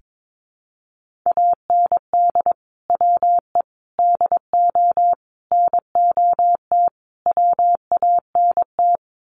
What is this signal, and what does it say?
Morse code, heard as a keyed tone, about 22 wpm, 700 Hz. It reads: AND WE DO NOT WANT
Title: Sets of 5 Words Encore2 22wpm